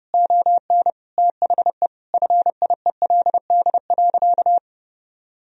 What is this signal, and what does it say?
Morse code, heard as a keyed tone, about 32 wpm, 700 Hz. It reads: ON THE FIELD.
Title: Sentences from Top 1000 Words - Review Part 2 - 30wpm